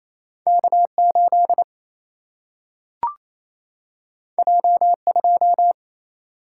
Morse code, heard as a keyed tone, about 28 wpm, 700 Hz. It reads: K8 J2
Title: Single-Letter Single-Number 28wpm